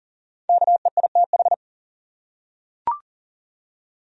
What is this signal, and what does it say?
Morse code, heard as a keyed tone, about 39 wpm, 700 Hz. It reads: KEITH
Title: USA Common Names - Max 5 Letters - 40wpm